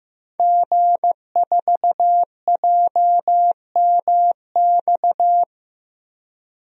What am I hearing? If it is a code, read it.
Morse code, heard as a keyed tone, about 15 wpm, 700 Hz. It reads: G4JMX
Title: DXCC-100 Call Signs - 5 Characters - Repeated 3x - 15wpm